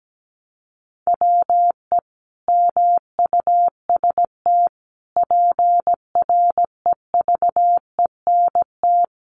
Morse code, heard as a keyed tone, about 17 wpm, 700 Hz. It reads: WE MUST PREVENT
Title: Sentences from Top 1100 Words 17wpm